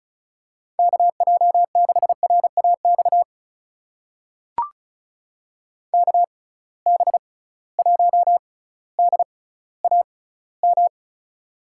Morse code, heard as a keyed tone, about 36 wpm, spaced as for 18 wpm, 700 Hz. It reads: KJ6RAX KB1DAM
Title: USA Call Signs - 6 Characters - 6x Character Spacing - 35wpm